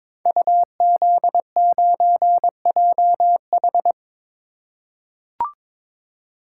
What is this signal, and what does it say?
Morse code, heard as a keyed tone, about 22 wpm, 700 Hz. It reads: UZ9JH
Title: Code Groups - 5 Letter:Numbers - 22wpm